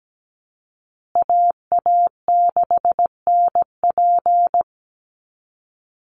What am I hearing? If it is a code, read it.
Morse code, heard as a keyed tone, about 17 wpm, 700 Hz. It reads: AA6NP
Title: USA Call Signs Repeated 3x 17wpm